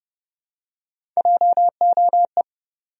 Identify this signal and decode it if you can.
Morse code, heard as a keyed tone, about 30 wpm, 700 Hz. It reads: JOE
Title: USA Common Names - Max 3 Letters - 30wpm